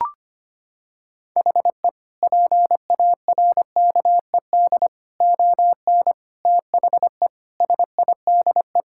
Morse code, heard as a keyed tone, about 25 wpm, 700 Hz. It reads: HE PARKED ON THE SIDE
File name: practice / 25wpm / Sentences with 4-Word Phrases - 25wpm